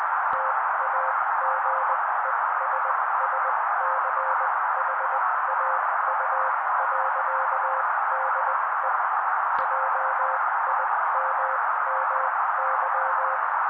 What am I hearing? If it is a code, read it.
Morse code, heard as a keyed tone, about 20 wpm, spaced as for 16 wpm, 530 Hz. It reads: TAGESSCHAU.DE JIMMY
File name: daily / 16wpm noise